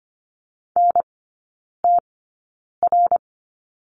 Morse code, heard as a keyed tone, about 25 wpm, spaced as for 9 wpm, 700 Hz. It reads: NTR
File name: Lesson 15e - Number 5 - Three Characters 25wpm